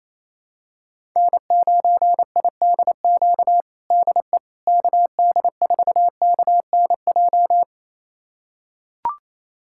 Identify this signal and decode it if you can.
Morse code, heard as a keyed tone, about 28 wpm, 700 Hz. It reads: N9IDQ DE KD4KNJ
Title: USA Callsign DE Callsign 28wpm